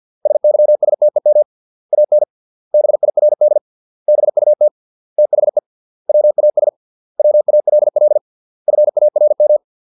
Morse code, heard as a keyed tone, about 50 wpm, 600 Hz. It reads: SYSTEM AN BIRD BUT THE WAS WALL FARM